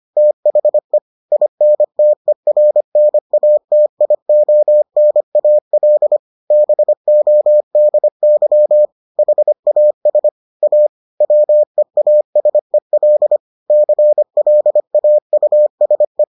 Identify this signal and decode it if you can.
Morse code, heard as a keyed tone, about 25 wpm, 600 Hz. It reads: THE INTERNATIONAL BODY HAS A WEASEL CLAUSE